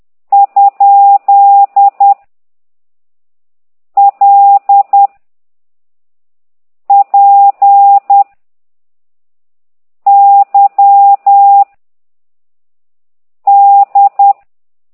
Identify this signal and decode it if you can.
Morse code, heard as a keyed tone, about 10 wpm, 800 Hz. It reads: ? L P Y D
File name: ct2idw / exerc9